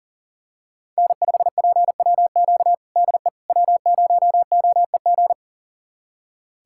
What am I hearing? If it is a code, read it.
Morse code, heard as a keyed tone, about 40 wpm, 700 Hz. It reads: N5PWQ DE W0OEG
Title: USA Callsign DE Callsign - 5 Characters - 40wpm